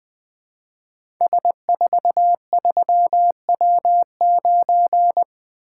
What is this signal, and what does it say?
Morse code, heard as a keyed tone, about 20 wpm, 700 Hz. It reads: S43W9
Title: Code Groups - 5 Letter:Numbers - 20wpm